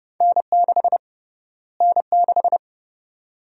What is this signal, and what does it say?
Morse code, heard as a keyed tone, about 32 wpm, 700 Hz. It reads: N6 N6
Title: USA Call Sign Prefixes - Repeated 2x - 30wpm